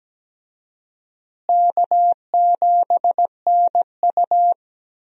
Morse code, heard as a keyed tone, about 17 wpm, 700 Hz. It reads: K7NU
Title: USA Call Signs - 4 Characters - Repeated 3x - 17wpm